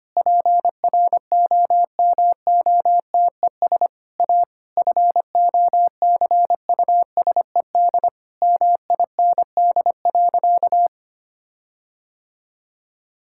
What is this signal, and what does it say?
Morse code, heard as a keyed tone, about 25 wpm, 700 Hz. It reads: PROMOTES A FOCUSED MIND.
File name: practice / 25wpm / Difficult Sentences with Hints - 25wpm